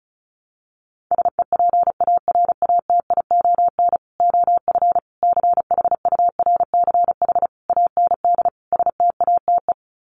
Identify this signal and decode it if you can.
Morse code, heard as a keyed tone, about 34 wpm, 700 Hz. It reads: SEPARATION OF CHURCH AND STATE
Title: Sets of 5 Words 35wpm